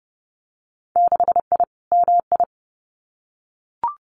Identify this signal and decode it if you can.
Morse code, heard as a keyed tone, about 32 wpm, 700 Hz. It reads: 6I MI
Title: Winter Field Day Exchanges Encore 30wpm